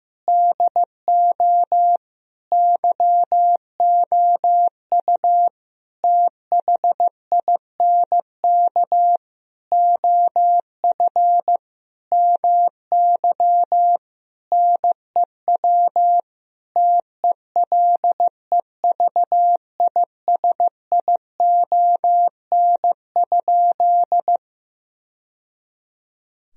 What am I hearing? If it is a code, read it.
Morse code, heard as a keyed tone, about 15 wpm, 700 Hz. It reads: DO YOU THINK OF MY NEW TELEVISION?
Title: Sentences from Top 1000 Words - Review Part 4 - 15wpm